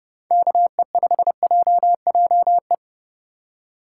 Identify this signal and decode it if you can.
Morse code, heard as a keyed tone, about 29 wpm, 700 Hz. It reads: KE5JJE